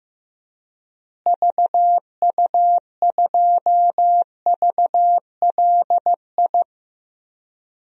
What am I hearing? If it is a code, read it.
Morse code, heard as a keyed tone, about 16 wpm, 700 Hz. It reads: VU2VLI
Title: DXCC-100 Call Signs - 6 Characters - Repeated 2x - 15wpm